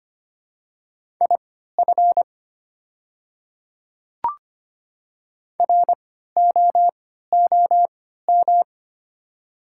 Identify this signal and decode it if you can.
Morse code, heard as a keyed tone, about 25 wpm, spaced as for 14 wpm, 700 Hz. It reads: IF ROOM
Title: 3x Character Spacing - Top 300 Words - 25wpm